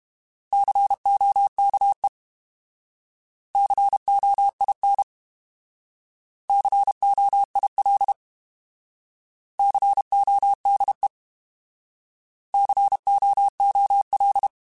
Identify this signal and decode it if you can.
Morse code, heard as a keyed tone, about 32 wpm, 800 Hz. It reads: COKE COIN COIL CODE COOL
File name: Alphabetical / Alphabetical mp3s / Cs @30